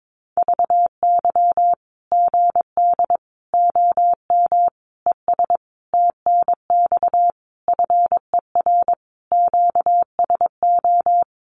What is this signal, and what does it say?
Morse code, heard as a keyed tone, about 22 wpm, 700 Hz. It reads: VY GD OM ES TNX FER QSO
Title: Common QSO Phrases 22wpm